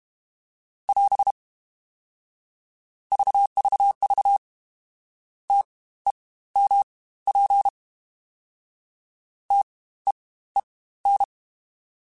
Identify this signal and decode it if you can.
Morse code, heard as a keyed tone, about 32 wpm, spaced as for 17 wpm, 800 Hz. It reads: <AS> VVV TEMP TEEN
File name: Spaced Alphas / Spaced Alphas mp3s / Ts sp @30